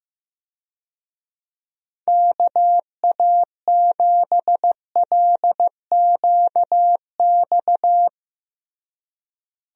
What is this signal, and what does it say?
Morse code, heard as a keyed tone, about 16 wpm, 700 Hz. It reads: KA7LQX
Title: USA Call Signs - 6 Characters - Repeated 2x - 15wpm